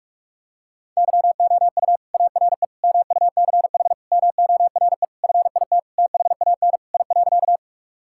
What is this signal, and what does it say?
Morse code, heard as a keyed tone, about 41 wpm, 700 Hz. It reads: YOU ARE MUCH MORE FIT THAN I.
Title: Sentences from Top 1000 Words - Review Part 5 - 45wpm